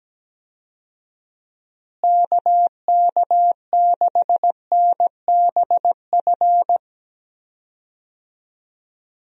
Morse code, heard as a keyed tone, about 17 wpm, 700 Hz. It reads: KK6NBF
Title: USA Call Signs - 6 Characters - Repeated 2x - 17wpm